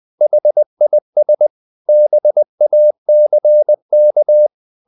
Morse code, heard as a keyed tone, about 20 wpm, 600 Hz. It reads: HIS BACK
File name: the old man and the sea part 2 20wpm